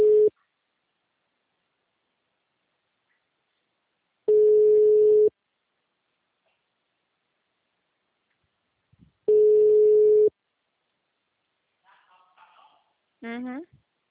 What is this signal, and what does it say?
Ringback tone (Europe, 425 Hz), with speech.